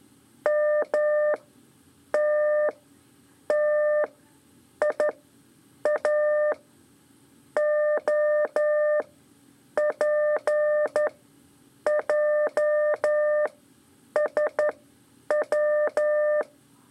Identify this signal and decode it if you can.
Morse code, heard as a keyed tone, about 11 wpm, spaced as for 7 wpm, 600 Hz. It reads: MTTIAOPJSW
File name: 3 lezione CW 2 AIR